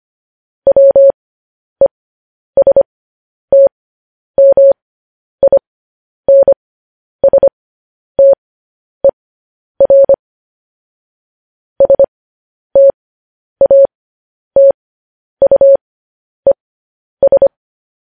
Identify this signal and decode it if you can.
Morse code, heard as a keyed tone, about 25 wpm, spaced as for 10 wpm, 550 Hz. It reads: WESTMINSTER STATUES
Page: News Headlines in Morse Code at 10 WPM